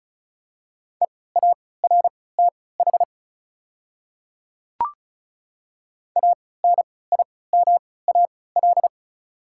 Morse code, heard as a keyed tone, about 36 wpm, spaced as for 20 wpm, 690 Hz. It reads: EARTH ANIMAL
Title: 3x Character Spacing - Top 500 Words - 35wpm